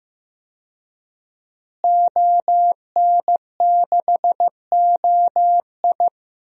Morse code, heard as a keyed tone, about 15 wpm, 700 Hz. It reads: ON6OI
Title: DXCC-100 Call Signs - 5 Characters - Repeated 2x - 15wpm